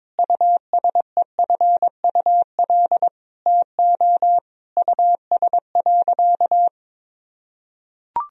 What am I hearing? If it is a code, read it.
Morse code, heard as a keyed tone, about 22 wpm, 700 Hz. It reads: USEFUL TO US.
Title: Sentences from Top 1000 Words - Review Part 4 - 22wpm